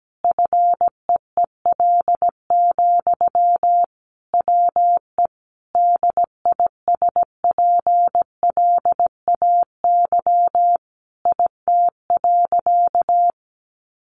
Morse code, heard as a keyed tone, about 17 wpm, 700 Hz. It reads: FEEL, WE DISPLAY IT.
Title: Sentences from Top 1200 Words 17wpm